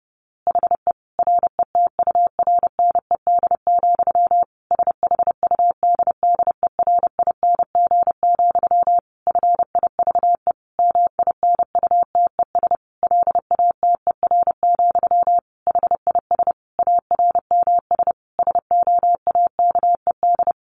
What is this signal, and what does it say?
Morse code, heard as a keyed tone, about 30 wpm, 700 Hz. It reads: HE RETURNED, SHUDDERING, FIVE MINUTES LATER, HIS ARMS SOAKED
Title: A Sound Of Thunder 30wpm